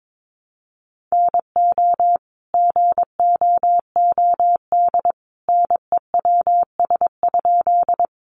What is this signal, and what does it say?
Morse code, heard as a keyed tone, about 22 wpm, 700 Hz. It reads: NO GOOD NEWS?
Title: Sentences from Top 800 Words 22wpm